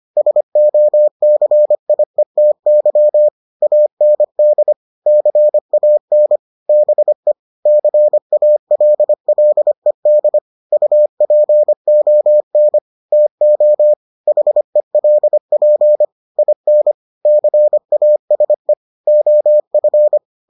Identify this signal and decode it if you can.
Morse code, heard as a keyed tone, about 25 wpm, 600 Hz. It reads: SOCIETY AND CAN BE CALLED UPON TO HELP IN CASE OF